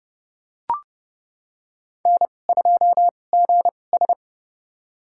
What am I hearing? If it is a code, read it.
Morse code, heard as a keyed tone, about 30 wpm, spaced as for 22 wpm, 700 Hz. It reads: N2GS
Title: USA Call Signs - 2x Character Spacing - 30wpm